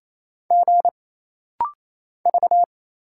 Morse code, heard as a keyed tone, about 29 wpm, 700 Hz. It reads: G V
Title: Single Letter-Number Alt - Rapid-Fire 28wpm